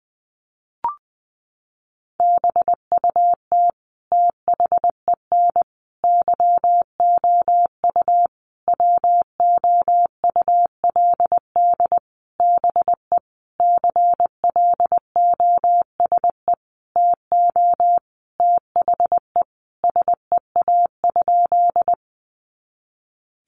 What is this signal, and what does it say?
Morse code, heard as a keyed tone, about 19 wpm, 700 Hz. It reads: BUT THEN YOU WOULD BE CLOSE TO THE SEA?
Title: Sentences from Top 300 Words 20wpm